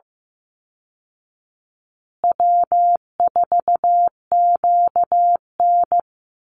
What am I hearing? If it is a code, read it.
Morse code, heard as a keyed tone, about 15 wpm, 700 Hz. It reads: W4QN
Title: Easy Call Signs Repeated 3x - 15wpm